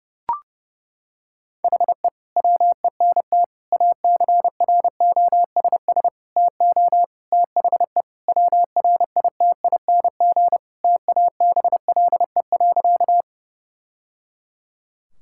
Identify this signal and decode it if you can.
Morse code, heard as a keyed tone, about 30 wpm, 700 Hz. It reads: HE WENT ACROSS TO THE WRITING TABLE.
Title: Sentences from Top 1000 Words - Review Part 5 - 30wpm